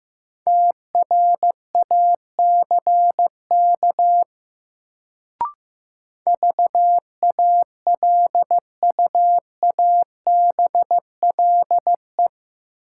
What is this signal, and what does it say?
Morse code, heard as a keyed tone, about 15 wpm, 700 Hz. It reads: TRACK VALUABLE
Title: Sentences from Top 1100 Words - New Word Drill Alt - 15wpm